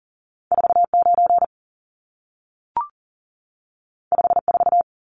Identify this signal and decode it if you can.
Morse code, heard as a keyed tone, about 40 wpm, 700 Hz. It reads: 49 54
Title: Numbers - 2-Digit 40wpm